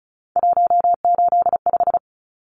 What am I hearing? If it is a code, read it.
Morse code, heard as a keyed tone, about 34 wpm, 700 Hz. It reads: 185
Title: Numbers - 3-Digit 35wpm